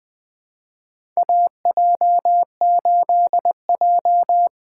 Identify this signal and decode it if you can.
Morse code, heard as a keyed tone, about 20 wpm, 700 Hz. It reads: AJ8J
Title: USA Call Signs - 4 Characters - 2x Character Spacing - 20wpm